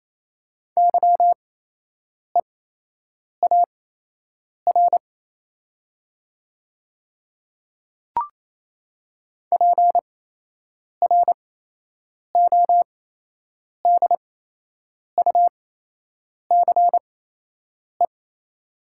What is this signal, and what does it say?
Morse code, heard as a keyed tone, about 28 wpm, spaced as for 8 wpm, 690 Hz. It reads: YEAR PRODUCE
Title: 8x Character Spacing - Top 500 Words - 28wpm